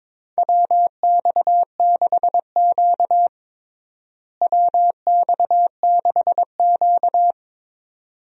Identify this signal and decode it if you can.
Morse code, heard as a keyed tone, about 22 wpm, 700 Hz. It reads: WX6Q WX6Q
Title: USA Call Signs - 4 Characters - Repeated 2x - 22wpm